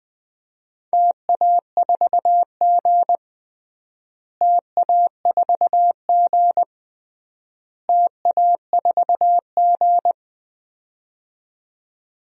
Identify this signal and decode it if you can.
Morse code, heard as a keyed tone, about 21 wpm, 700 Hz. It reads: TA4G TA4G TA4G
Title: DXCC-100 Call Signs - 4 Characters - Repeated 3x - 20wpm